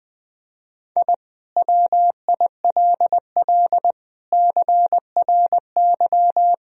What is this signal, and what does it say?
Morse code, heard as a keyed tone, about 20 wpm, 700 Hz. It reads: I WILL CRY
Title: Sentences from Top 1000 Words - Review Part 2 - 20wpm